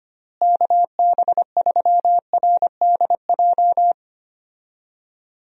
Morse code, heard as a keyed tone, about 25 wpm, 700 Hz. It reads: KB3RDJ
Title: USA Call Signs - 6 Characters - Repeated 2x - 25wpm